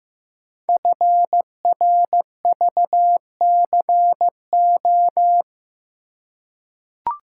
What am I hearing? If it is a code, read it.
Morse code, heard as a keyed tone, about 15 wpm, 700 Hz. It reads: FRVCO
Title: Code Groups - 5 Letters - 15wpm